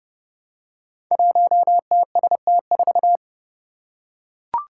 Morse code, heard as a keyed tone, about 30 wpm, 700 Hz. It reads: 1TST4
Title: Five Challenging Letter-Number Combinations 30wpm